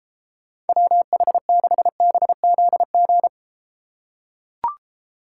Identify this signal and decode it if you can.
Morse code, heard as a keyed tone, about 31 wpm, 700 Hz. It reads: WH6BZG